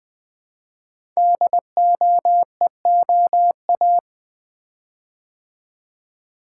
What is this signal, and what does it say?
Morse code, heard as a keyed tone, about 20 wpm, 700 Hz. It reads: DOEOA
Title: Code Groups - 5 Letters - 20wpm